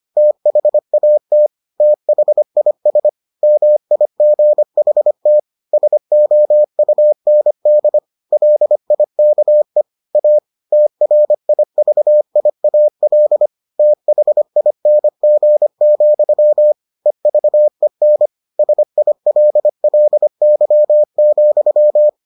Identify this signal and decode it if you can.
Morse code, heard as a keyed tone, about 25 wpm, 600 Hz. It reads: THAT THIS MIGHT SOUND LIKE A TRIVIAL THING, EVEN SILLY,